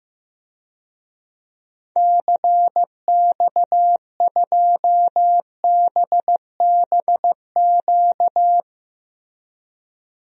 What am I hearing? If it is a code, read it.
Morse code, heard as a keyed tone, about 15 wpm, 700 Hz. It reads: CX2BBQ
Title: DXCC-100 Call Signs - 6 Characters - 15wpm